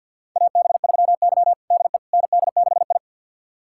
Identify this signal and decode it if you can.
Morse code, heard as a keyed tone, about 51 wpm, 700 Hz. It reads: AB3Y DE ND6I
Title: USA Callsign DE Callsign - 4 Characters - 50wpm